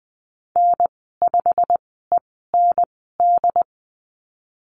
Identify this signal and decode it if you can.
Morse code, heard as a keyed tone, about 20 wpm, spaced as for 15 wpm, 700 Hz. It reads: N5END
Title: 173 Lesson 28g - Number 9 - Call Signs 20wpm